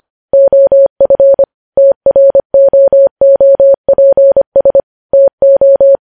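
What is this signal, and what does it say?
Morse code, heard as a keyed tone, about 25 wpm, 550 Hz. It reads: OF TROOPS TO
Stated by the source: News headlines sent in Morse code at 25 wpm.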